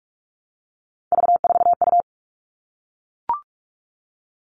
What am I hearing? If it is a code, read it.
Morse code, heard as a keyed tone, about 44 wpm, 700 Hz. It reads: V4U
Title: All Cut Numbers - 3-Digit 45wpm